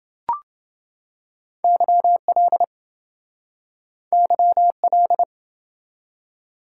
Morse code, heard as a keyed tone, about 27 wpm, 700 Hz. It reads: YL YL